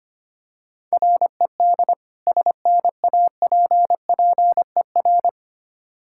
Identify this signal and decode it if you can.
Morse code, heard as a keyed tone, about 25 wpm, 700 Hz. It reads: RED SNAPPER
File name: practice / 25wpm / Difficult Words with Hints - 25wpm